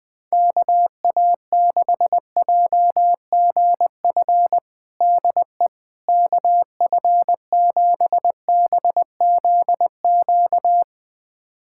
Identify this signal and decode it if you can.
Morse code, heard as a keyed tone, about 20 wpm, 700 Hz. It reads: KA6JGF DE KF7BZQ
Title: USA Callsign DE Callsign - 6 Characters - 20wpm